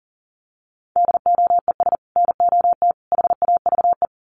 Morse code, heard as a keyed tone, about 40 wpm, 700 Hz. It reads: DOES NOT HAVE
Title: Sets of 3 Words 40wpm